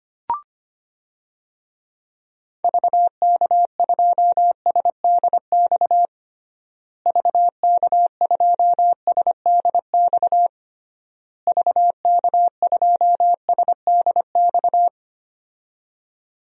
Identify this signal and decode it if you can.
Morse code, heard as a keyed tone, about 25 wpm, 700 Hz. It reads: VK2SDX VK2SDX VK2SDX
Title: DXCC-100 Call Signs - 6 Characters - Repeated 3x - 25wpm